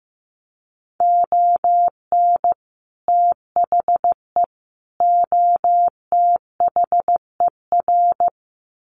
Morse code, heard as a keyed tone, about 15 wpm, 700 Hz. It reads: ON THE OTHER
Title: Sets of 3 Words 15wpm